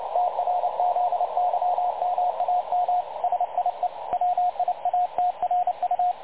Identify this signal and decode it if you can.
Morse code, heard as a keyed tone, about 29 wpm, 720 Hz, with radio noise. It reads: TRZYMAM SIE WIATRU
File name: dlazaawansowanych / nagraniazpasma / websdr recording 3565.0kHz